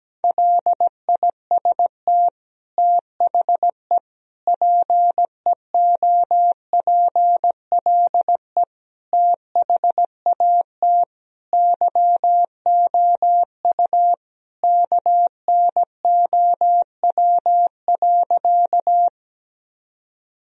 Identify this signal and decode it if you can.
Morse code, heard as a keyed tone, about 17 wpm, 700 Hz. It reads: LIST THE PEOPLE THAT YOU KNOW.
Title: Sentences from Top 1000 Words - Review Part 2 - 17wpm